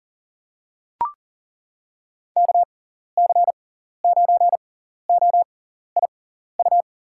Morse code, heard as a keyed tone, about 37 wpm, spaced as for 14 wpm, 700 Hz. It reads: KC9OIU
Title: USA Call Signs - 6 Characters - 6x Character Spacing - 40wpm